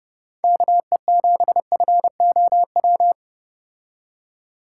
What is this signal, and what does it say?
Morse code, heard as a keyed tone, about 31 wpm, 700 Hz. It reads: KE7FOW